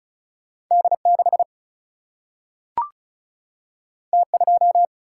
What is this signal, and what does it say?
Morse code, heard as a keyed tone, about 36 wpm, 700 Hz. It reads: D6 T2
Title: Single-Letter Single-Number 35wpm